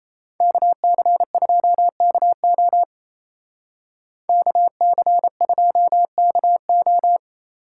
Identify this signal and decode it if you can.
Morse code, heard as a keyed tone, about 32 wpm, 700 Hz. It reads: KC2KO KC2KO
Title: Speed Racing - USA 2x2 Call Signs 28wpm